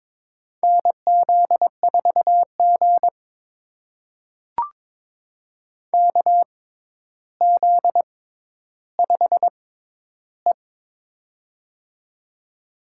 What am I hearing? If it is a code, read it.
Morse code, heard as a keyed tone, about 22 wpm, spaced as for 11 wpm, 700 Hz. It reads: NZ4G KZ5E
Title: USA Call Signs - 4 Characters - 6x Character Spacing - 22wpm